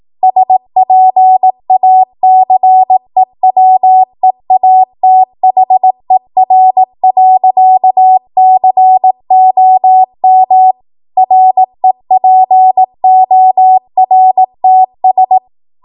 Morse code, heard as a keyed tone, about 18 wpm, 750 Hz. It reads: SPACEWEATHER.COM REPORTS
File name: Morse / 18WPM